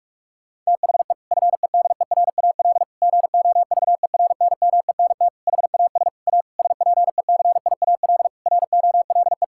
Morse code, heard as a keyed tone, about 46 wpm, 700 Hz. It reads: THE FEDERAL GOVERNMENT HAS A SPECIAL ROLE
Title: Sentences from Top 1000 Words - Review Part 6 - 45wpm